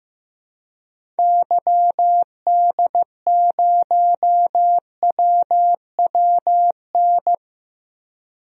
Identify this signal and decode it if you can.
Morse code, heard as a keyed tone, about 15 wpm, 700 Hz. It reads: YD0WWN
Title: DXCC-100 Call Signs - 6 Characters - Repeated 3x - 15wpm